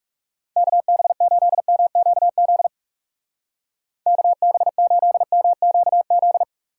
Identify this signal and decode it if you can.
Morse code, heard as a keyed tone, about 38 wpm, 700 Hz. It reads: KB8MQZ KB8MQZ